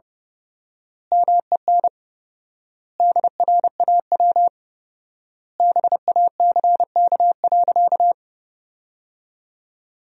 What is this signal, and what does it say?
Morse code, heard as a keyed tone, about 30 wpm, 700 Hz. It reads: MEN DRAW BACK.
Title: 4x Word-Spacing - Sentences from Top 300 Words - 30wpm